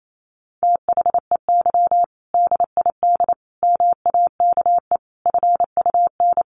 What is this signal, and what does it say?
Morse code, heard as a keyed tone, about 28 wpm, 700 Hz. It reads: THEY DID MAKE FUN
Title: Sentences from Top 1300 Words 28wpm